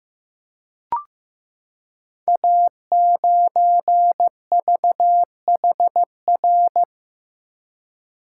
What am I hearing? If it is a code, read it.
Morse code, heard as a keyed tone, about 15 wpm, 700 Hz. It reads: A9VHR
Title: Code Groups - 5 Letter:Numbers - 15wpm